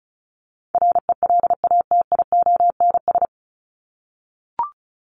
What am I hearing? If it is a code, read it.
Morse code, heard as a keyed tone, about 34 wpm, 700 Hz. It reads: RELATIONS
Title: ES:S Suffix - Top 100 Words with Base Words 35wpm